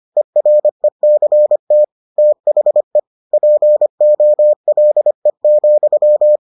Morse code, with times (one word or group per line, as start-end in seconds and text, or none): 0.2-1.8 ERECT
2.2-3.0 THE
3.3-6.4 POLE,